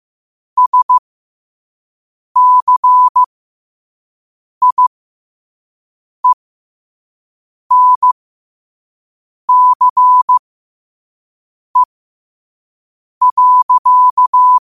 Morse code, heard as a keyed tone, about 15 wpm, spaced as for 5 wpm, 1000 Hz. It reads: SCIENCE.
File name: qotdfiles / 5wpm